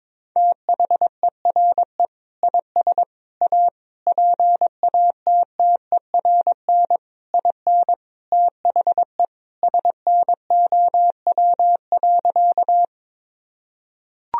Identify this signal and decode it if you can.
Morse code, heard as a keyed tone, about 22 wpm, 700 Hz. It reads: THERE IS A PATTERN IN THE SNOW.
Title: Sentences from Top 1000 Words - Review Part 2 - 22wpm